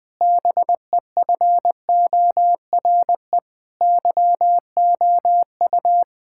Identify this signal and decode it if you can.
Morse code, heard as a keyed tone, about 21 wpm, 700 Hz. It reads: BEFORE YOU
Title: Difficult Sentences with Hints - 20wpm